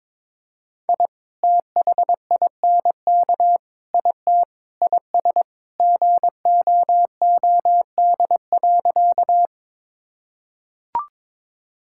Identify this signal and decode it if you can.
Morse code, heard as a keyed tone, about 22 wpm, 700 Hz. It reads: I THINK IT IS GOOD.
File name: practice / 22wpm / Sentences with 3-Word Phrases - 22wpm